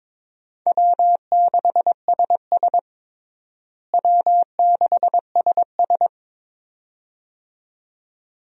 Morse code, heard as a keyed tone, about 22 wpm, 700 Hz. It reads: W6SS W6SS
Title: USA Call Signs - 4 Characters - Repeated 2x - 22wpm